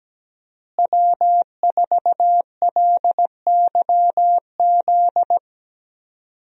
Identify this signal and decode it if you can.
Morse code, heard as a keyed tone, about 17 wpm, 700 Hz. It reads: W4LYZ
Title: USA Call Signs - 5 Characters - Repeated 3x - 17wpm